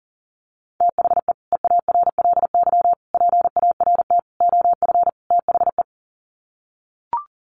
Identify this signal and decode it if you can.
Morse code, heard as a keyed tone, about 40 wpm, 700 Hz. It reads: THE EARLY PART OF THE
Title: Sets of 5 Words Encore2 40wpm